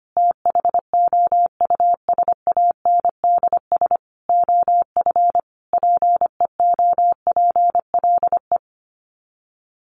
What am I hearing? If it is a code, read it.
Morse code, heard as a keyed tone, about 25 wpm, 700 Hz. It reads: THOUSANDS OF PEOPLE
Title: Sets of 5 Words Encore2 25wpm